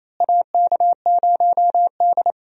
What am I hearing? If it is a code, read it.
Morse code, heard as a keyed tone, about 28 wpm, 700 Hz. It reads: AK0D
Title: USA Call Signs - 4 Characters - Repeated 3x - 28wpm